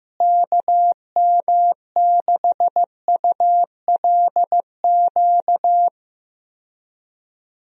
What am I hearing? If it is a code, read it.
Morse code, heard as a keyed tone, about 15 wpm, 700 Hz. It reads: KM6ULQ
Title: USA Call Signs - 6 Characters - Repeated 3x - 15wpm